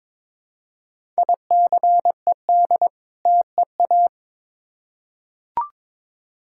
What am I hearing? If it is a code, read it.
Morse code, heard as a keyed tone, about 21 wpm, 700 Hz. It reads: ICED TEA
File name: Difficult Words with Hints - 22wpm